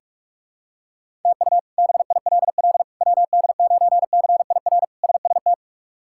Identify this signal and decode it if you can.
Morse code, heard as a keyed tone, about 42 wpm, 700 Hz. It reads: TU BILL WD9CIR SST
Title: K1USN SST QSO Practice - 45wpm